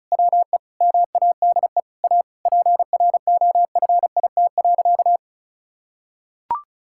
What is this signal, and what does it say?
Morse code, heard as a keyed tone, about 36 wpm, 690 Hz. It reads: WE MADE A PROFIT.
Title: Sentences from Top 1000 Words - Review Part 5 - 35wpm